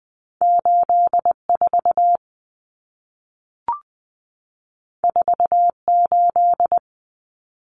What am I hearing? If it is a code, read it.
Morse code, heard as a keyed tone, about 20 wpm, 700 Hz. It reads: 84 48
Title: Common Cut Numbers - 2-Digit 20wpm